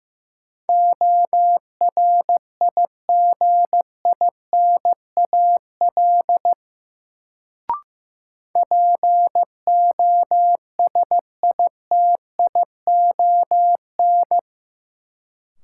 Morse code, heard as a keyed tone, about 15 wpm, 700 Hz. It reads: ORIGINAL POSITION
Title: Sentences from Top 900 Words - New Word Drill - 15wpm